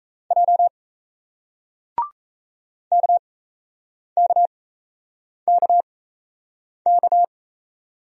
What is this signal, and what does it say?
Morse code, heard as a keyed tone, about 37 wpm, spaced as for 8 wpm, 700 Hz. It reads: J KKKK